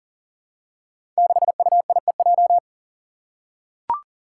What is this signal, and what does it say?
Morse code, heard as a keyed tone, about 38 wpm, 700 Hz. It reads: 6UIEJ
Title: Five Challenging Letter-Number Combinations 40wpm